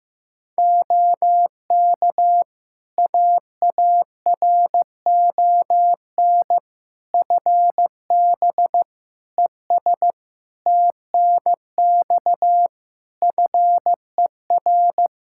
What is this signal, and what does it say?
Morse code, heard as a keyed tone, about 15 wpm, 700 Hz. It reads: OK AARON FB ES TNX FER
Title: Practice QSO - Exchange 2 - Info - 15wpm